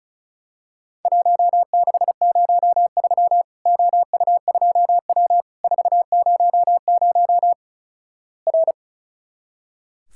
Morse code, heard as a keyed tone, about 36 wpm, 700 Hz. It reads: 1603 OU2W 400 R
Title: Worked All Europe DX Contest - QTC Exchange - 35wpm